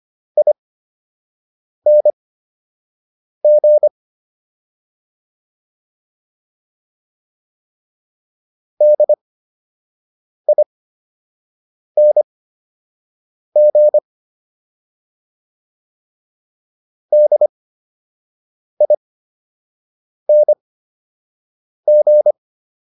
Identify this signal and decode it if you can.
Morse code, heard as a keyed tone, about 25 wpm, spaced as for 6 wpm, 600 Hz. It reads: ING DING DING